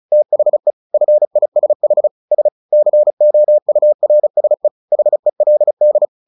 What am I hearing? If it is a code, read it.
Morse code, heard as a keyed tone, about 34 wpm, 600 Hz. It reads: THE FISH S COURSE HELD
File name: the old man and the sea / the old man and the sea 1 35wpm